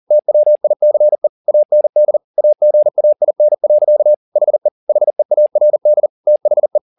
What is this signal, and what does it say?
Morse code, heard as a keyed tone, about 40 wpm, 600 Hz. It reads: TWICE AND AGAIN. HE HEARD THE
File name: the old man and the sea 2 40wpm